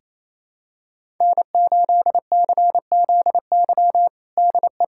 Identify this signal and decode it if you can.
Morse code, heard as a keyed tone, about 28 wpm, 700 Hz. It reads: N8CZY DE
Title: USA Callsign DE Callsign - 5 Characters - 28wpm